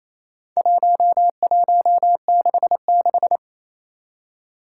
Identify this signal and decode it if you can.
Morse code, heard as a keyed tone, about 28 wpm, 700 Hz. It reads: 1166
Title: Four Challenging Letter-Number Combinations 28wpm